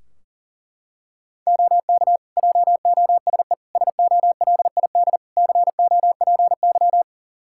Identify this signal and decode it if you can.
Morse code, heard as a keyed tone, about 40 wpm, 700 Hz. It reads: OK JOSE SOLID COPY
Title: K1USN SST QSO Practice 40wpm